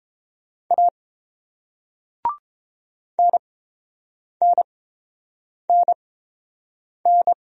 Morse code, heard as a keyed tone, about 30 wpm, spaced as for 7 wpm, 700 Hz. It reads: A NNNN